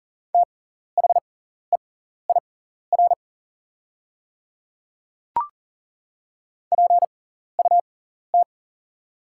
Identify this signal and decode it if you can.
Morse code, heard as a keyed tone, about 40 wpm, spaced as for 14 wpm, 700 Hz. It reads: THEIR PUT
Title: 6x Character Spacing - Top 200 Words - 40wpm